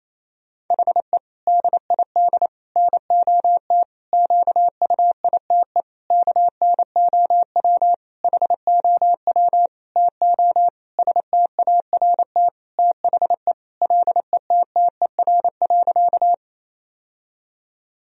Morse code, heard as a keyed tone, about 28 wpm, 700 Hz. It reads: HE DID NOT QUITE KNOW HOW TO START THE LETTER.
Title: Sentences from Top 1000 Words - Review Part 1 - 28wpm